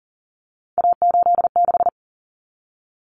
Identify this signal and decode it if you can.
Morse code, heard as a keyed tone, about 40 wpm, 700 Hz. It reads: A86
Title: Common Cut Numbers - 3-Digit 40wpm